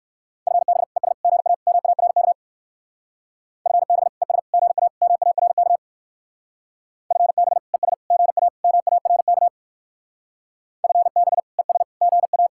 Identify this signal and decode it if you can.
Morse code, heard as a keyed tone, about 58 wpm, 700 Hz. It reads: FB ES GA MARK FB ES GA MARK FB ES GA MARK FB ES GA